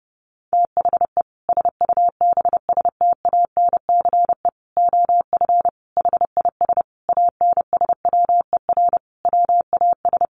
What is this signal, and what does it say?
Morse code, heard as a keyed tone, about 30 wpm, 700 Hz. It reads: THE SUBSTANCE OF HIS ANSWER WAS
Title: Sentences from Top 1500 Words 30wpm